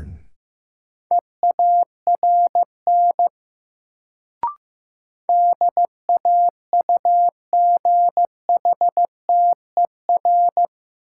Morse code, heard as a keyed tone, about 15 wpm, 700 Hz. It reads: EARN DAUGHTER
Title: Sentences from Top 1500 Words - New Word Drill - 15wpm